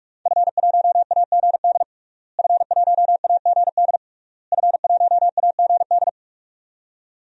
Morse code, heard as a keyed tone, about 46 wpm, 700 Hz. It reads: F1AGD F1AGD F1AGD
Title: DXCC-100 Call Signs - 5 Characters - Repeated 3x - 45wpm